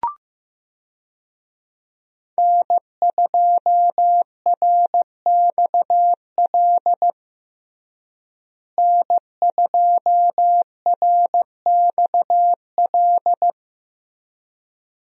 Morse code, heard as a keyed tone, about 15 wpm, 700 Hz. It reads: N2RXL N2RXL
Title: USA Call Signs - 5 Characters - Repeated 2x - 15wpm